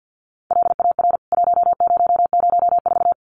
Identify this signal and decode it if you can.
Morse code, heard as a keyed tone, about 49 wpm, 700 Hz. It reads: LAR 1004
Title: CWT Exchange 50wpm